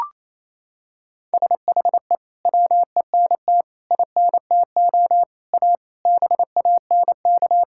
Morse code, heard as a keyed tone, about 27 wpm, 700 Hz. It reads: SHE WENT INTO A BANK
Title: Sentences from Top 1000 Words - Review Part 6 - 28wpm